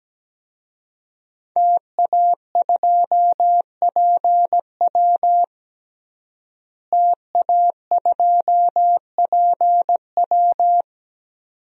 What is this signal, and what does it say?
Morse code, heard as a keyed tone, about 17 wpm, 700 Hz. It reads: TA2PW TA2PW
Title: DXCC-100 Call Signs - 5 Characters - Repeated 2x - 17wpm